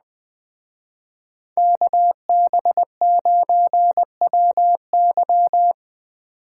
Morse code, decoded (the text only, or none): KB9WY